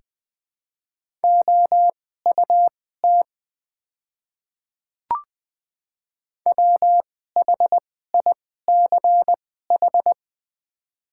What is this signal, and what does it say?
Morse code, heard as a keyed tone, about 21 wpm, spaced as for 15 wpm, 700 Hz. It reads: OUT WHICH
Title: 2x Character Spacing - Top 500 Words - 20wpm